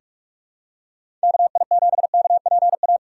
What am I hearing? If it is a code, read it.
Morse code, heard as a keyed tone, about 46 wpm, 700 Hz. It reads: KI7KPA
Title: USA Call Signs - 6 Characters - 3x Character Spacing - 45wpm